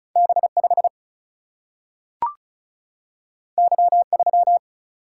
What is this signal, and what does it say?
Morse code, heard as a keyed tone, about 36 wpm, 700 Hz. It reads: B 5 Y 3
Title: Single-Letter Single-Number 35wpm